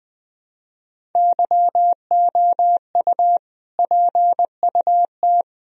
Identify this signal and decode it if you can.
Morse code, heard as a keyed tone, about 21 wpm, 700 Hz. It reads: YOU PUT
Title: Sentences from Top 1000 Words - Review Part 1 - 20wpm